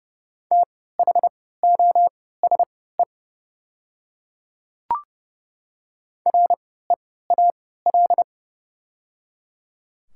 Morse code, heard as a keyed tone, about 30 wpm, spaced as for 17 wpm, 700 Hz. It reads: THOSE REAL
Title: 3x Character Spacing - Top 300 Words - 30wpm